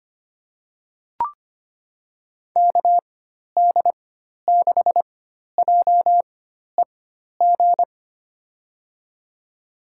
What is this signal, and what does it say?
Morse code, heard as a keyed tone, about 25 wpm, spaced as for 12 wpm, 700 Hz. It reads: KD6JEG
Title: USA Call Signs - 6 Characters - 4x Character Spacing - 25wpm